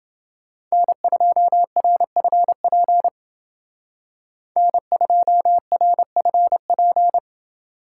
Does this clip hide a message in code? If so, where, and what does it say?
Yes: Morse code from 0.7 to 7.2 s, heard as a keyed tone, about 28 wpm, 700 Hz. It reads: N2RFP N2RFP